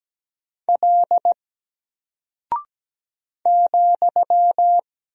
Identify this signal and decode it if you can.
Morse code, heard as a keyed tone, about 17 wpm, 700 Hz. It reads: L ,
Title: Single Character (Letter:Number:Punctuation:Prosign) 17wpm